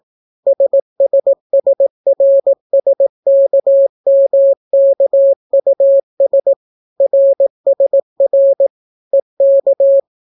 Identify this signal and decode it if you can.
Morse code, heard as a keyed tone, about 18 wpm, 550 Hz. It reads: SSSRSKMKUS RSR EK